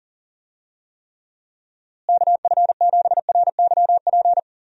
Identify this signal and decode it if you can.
Morse code, heard as a keyed tone, about 40 wpm, 700 Hz. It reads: KF7RYP
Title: USA Call Signs - 6 Characters - Repeated 3x - 40wpm